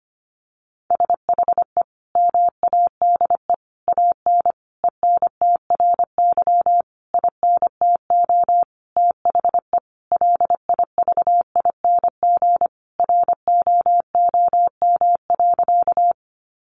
Morse code, heard as a keyed tone, about 25 wpm, 700 Hz. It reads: SHE MADE AN ENTRY INTO THE LIVING ROOM.
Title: Sentences from Top 1500 Words 25wpm